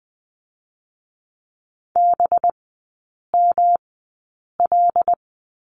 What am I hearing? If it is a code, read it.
Morse code, heard as a keyed tone, about 20 wpm, spaced as for 8 wpm, 700 Hz. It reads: BML
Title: 184 Lesson 30e - Letter b - Three Characters 20wpm